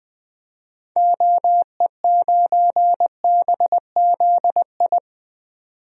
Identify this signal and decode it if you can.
Morse code, heard as a keyed tone, about 20 wpm, 700 Hz. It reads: OE9BZI
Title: DXCC-100 Call Signs - 6 Characters - 20wpm